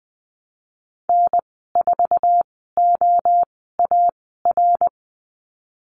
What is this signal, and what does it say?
Morse code, heard as a keyed tone, about 20 wpm, spaced as for 15 wpm, 700 Hz. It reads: N4OAR
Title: 055 Lesson 10g - Letter r - Call Signs 20wpm